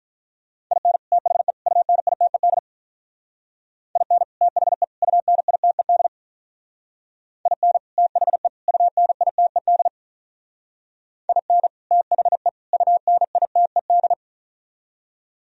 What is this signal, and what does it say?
Morse code, heard as a keyed tone, about 45 wpm, 700 Hz. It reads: IN THE UNITED IN THE UNITED IN THE UNITED IN THE UNITED